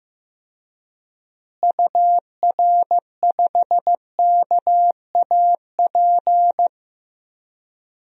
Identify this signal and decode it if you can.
Morse code, heard as a keyed tone, about 15 wpm, 700 Hz. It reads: UR5KAP
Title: DXCC-100 Call Signs - 6 Characters - Repeated 3x - 15wpm